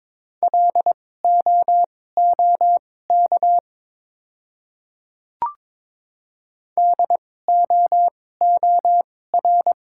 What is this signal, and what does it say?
Morse code, heard as a keyed tone, about 22 wpm, spaced as for 16 wpm, 700 Hz. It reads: LOOK DOOR
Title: 2x Character Spacing - Top 500 Words - 22wpm